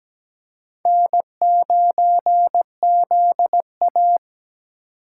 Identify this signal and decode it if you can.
Morse code, heard as a keyed tone, about 17 wpm, 700 Hz. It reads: N9ZA
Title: USA Call Signs - 4 Characters - 4x Character Spacing - 17wpm